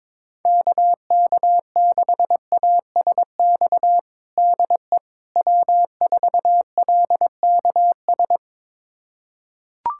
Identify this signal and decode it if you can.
Morse code, heard as a keyed tone, about 21 wpm, 700 Hz. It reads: KK6ASX DE W4LKS
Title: USA Callsign DE Callsign 22wpm